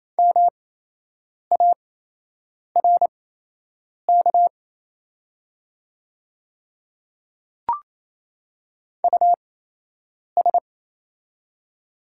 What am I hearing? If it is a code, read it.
Morse code, heard as a keyed tone, about 27 wpm, spaced as for 8 wpm, 700 Hz. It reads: MARK US
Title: 8x Character Spacing - Top 300 Words - 28wpm